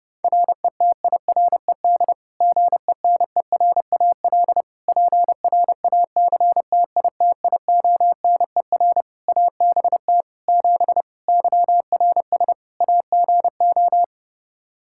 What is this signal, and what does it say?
Morse code, heard as a keyed tone, about 30 wpm, 700 Hz. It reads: RETIRED GENERAL PRACTITIONER ABT 7 YRS AGO
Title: Practice QSO - Exchange 3 - More Info - 30wpm